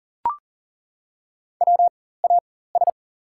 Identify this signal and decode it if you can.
Morse code, heard as a keyed tone, about 38 wpm, spaced as for 18 wpm, 700 Hz. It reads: WAS
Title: 4x Character Spacing - Top 100 Words - 40wpm